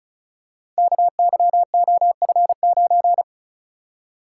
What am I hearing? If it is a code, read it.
Morse code, heard as a keyed tone, about 35 wpm, 700 Hz. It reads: KYOF9
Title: Code Groups - 5 Letter:Numbers - 35wpm